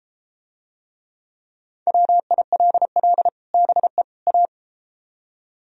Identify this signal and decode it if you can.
Morse code, heard as a keyed tone, about 34 wpm, 700 Hz. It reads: WILL BE A